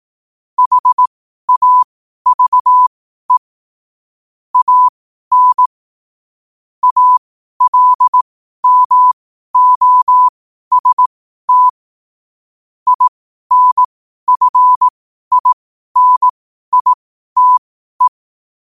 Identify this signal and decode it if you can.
Morse code, heard as a keyed tone, about 18 wpm, spaced as for 12 wpm, 1000 Hz. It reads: HAVE AN ALMOST INFINITE
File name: qotdfiles / 12wpm